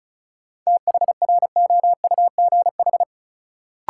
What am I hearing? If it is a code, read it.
Morse code, heard as a keyed tone, about 36 wpm, 700 Hz. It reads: THROUGH
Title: Sentences from Top 200 Words - New Word Drill - 35wpm